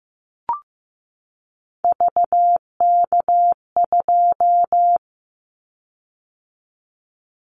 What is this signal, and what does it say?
Morse code, heard as a keyed tone, about 15 wpm, 700 Hz. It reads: VK2
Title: Callsign Prefixes for Top 100 DXCC - 15wpm